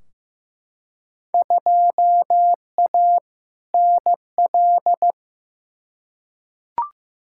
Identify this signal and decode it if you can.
Morse code, heard as a keyed tone, about 16 wpm, 700 Hz. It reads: 2A NL
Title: Field Day Exchanges Encore - 15wpm